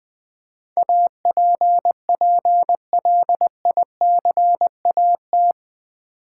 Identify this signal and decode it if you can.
Morse code, heard as a keyed tone, about 20 wpm, 700 Hz. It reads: APPLICAT
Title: Build A Word - 4 Letters At A Time - 11-Letter Words - 20wpm